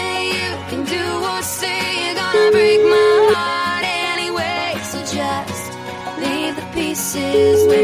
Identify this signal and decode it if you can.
Ringback tone (Europe, 425 Hz), with noise.